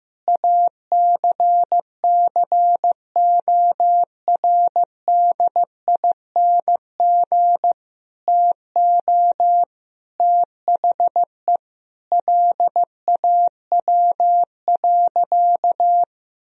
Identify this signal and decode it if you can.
Morse code, heard as a keyed tone, about 15 wpm, 700 Hz. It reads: ACCORDING TO THE LAW.
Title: Sentences from Top 1000 Words - Review Part 4 - 15wpm